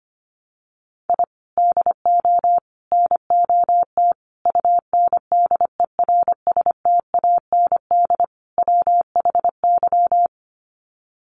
Morse code, heard as a keyed tone, about 25 wpm, 700 Hz. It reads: I DO NOT UNDERSTAND WHY
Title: Sets of 5 Words Encore 25wpm